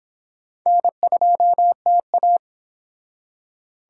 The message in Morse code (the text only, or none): N2TA